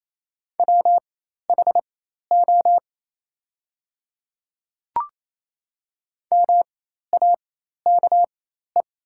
Morse code, heard as a keyed tone, about 28 wpm, spaced as for 13 wpm, 700 Hz. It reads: WHO MAKE
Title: 4x Character Spacing - Top 100 Words - 28wpm